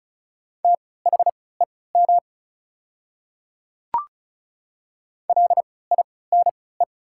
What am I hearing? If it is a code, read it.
Morse code, heard as a keyed tone, about 36 wpm, spaced as for 20 wpm, 700 Hz. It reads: THEM LINE
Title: 3x Character Spacing - Top 200 Words - 35wpm